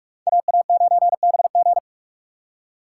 Morse code, heard as a keyed tone, about 58 wpm, 700 Hz. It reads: AA9BG